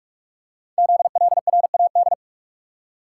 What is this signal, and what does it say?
Morse code, heard as a keyed tone, about 44 wpm, 700 Hz. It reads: ZLRAD
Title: Code Groups - 5 Letter:Numbers - 45wpm